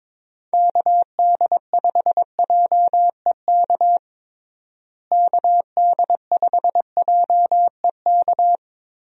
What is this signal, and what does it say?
Morse code, heard as a keyed tone, about 22 wpm, 700 Hz. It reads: KD5JEK KD5JEK
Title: USA Call Signs - 6 Characters - Repeated 3x - 22wpm